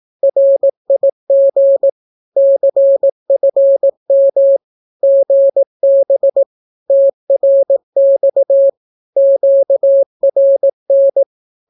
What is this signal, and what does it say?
Morse code, heard as a keyed tone, about 18 wpm, 550 Hz. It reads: RIG CFM GB TRX QRN
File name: naukapodstaw / skroty / skroty